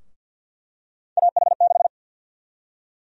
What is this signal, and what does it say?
Morse code, heard as a keyed tone, about 52 wpm, 700 Hz. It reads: AH6
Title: USA Call Sign Prefixes - 50wpm